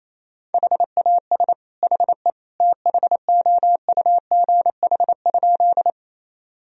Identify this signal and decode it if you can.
Morse code, heard as a keyed tone, about 28 wpm, 700 Hz. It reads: HAS HE THOUGH?
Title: Sentences from Top 1000 Words - Review Part 2 - 28wpm